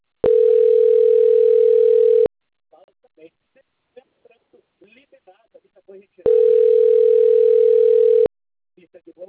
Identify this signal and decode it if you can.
Ringback tone (North America), 440 + 480 Hz, with speech.
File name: rg-1600-unknown- - - .21177